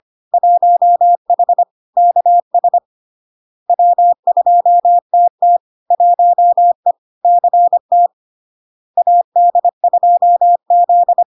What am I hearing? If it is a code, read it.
Morse code, heard as a keyed tone, about 25 wpm, 700 Hz. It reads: 1H KS W2TT 1E CT AD2Z
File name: LARK-FD-2022-Log-in-CW-25wpm-700-Hz